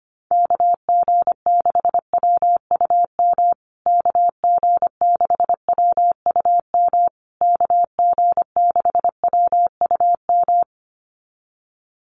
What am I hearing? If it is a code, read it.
Morse code, heard as a keyed tone, about 25 wpm, 700 Hz. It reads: KG6WUM KG6WUM KG6WUM
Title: USA Call Signs Repeated 3x 25wpm